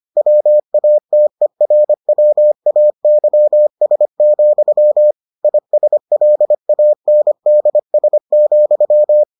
Morse code, heard as a keyed tone, about 25 wpm, 600 Hz. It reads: WATERWAYS, ISLANDS,